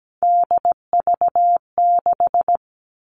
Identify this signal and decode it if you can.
Morse code, heard as a keyed tone, about 17 wpm, 700 Hz. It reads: DV6
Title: All Cut Numbers - 3-Digit 17wpm